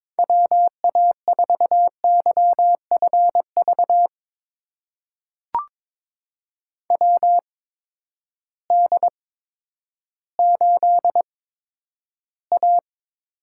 Morse code, heard as a keyed tone, about 22 wpm, spaced as for 11 wpm, 700 Hz. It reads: WA4YFV WD8A
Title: USA Call Signs - 6 Characters - 8x Character Spacing - 22wpm